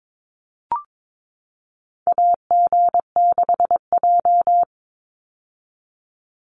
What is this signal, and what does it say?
Morse code, heard as a keyed tone, about 22 wpm, 700 Hz. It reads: AG6J
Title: USA Call Signs 22wpm